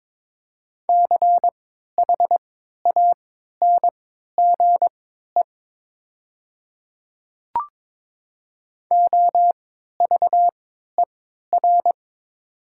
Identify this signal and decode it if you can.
Morse code, heard as a keyed tone, about 22 wpm, spaced as for 12 wpm, 700 Hz. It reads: CHANGE OVER
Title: 3x Character Spacing - Top 300 Words - 22wpm